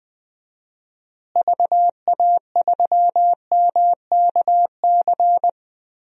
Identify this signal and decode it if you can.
Morse code, heard as a keyed tone, about 21 wpm, 700 Hz. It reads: VA3MKC
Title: DXCC-100 Call Signs - 6 Characters - Repeated 3x - 20wpm